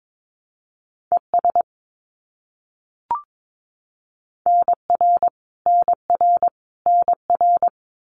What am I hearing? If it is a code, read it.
Morse code, heard as a keyed tone, about 22 wpm, 700 Hz. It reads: ES NR NR NR
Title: Abbreviations Repeated 3x - 22wpm